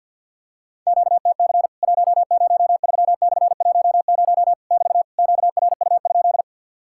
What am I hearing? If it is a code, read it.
Morse code, heard as a keyed tone, about 50 wpm, 700 Hz. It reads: QTC 103/10 <BT> QRU?
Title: Worked All Europe DX Contest - QTC Exchange - 50wpm